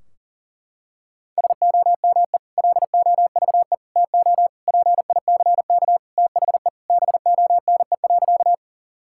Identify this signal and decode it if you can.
Morse code, heard as a keyed tone, about 40 wpm, 700 Hz. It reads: SOME LOVE TO PICK THE BONE.
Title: Sentences from Top 1600 Words 40wpm